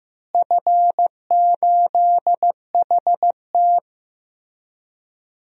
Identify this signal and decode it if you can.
Morse code, heard as a keyed tone, about 15 wpm, 700 Hz. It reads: F8HT
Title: DXCC-100 Call Signs - 4 Characters - Repeated 3x - 15wpm